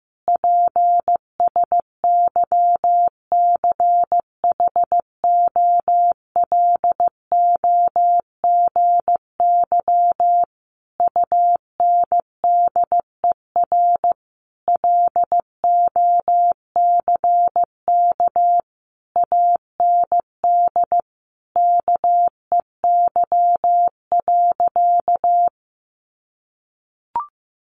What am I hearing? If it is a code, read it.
Morse code, heard as a keyed tone, about 14 wpm, 700 Hz. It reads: PSYCHOLOGY UNDER LOCK AND KEY.
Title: Sentences from Top 1400 Words 15wpm